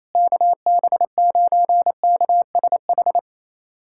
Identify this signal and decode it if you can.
Morse code, heard as a keyed tone, about 27 wpm, 700 Hz. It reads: KB9KSH
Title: USA Callsign DE Callsign - 6 Characters - 28wpm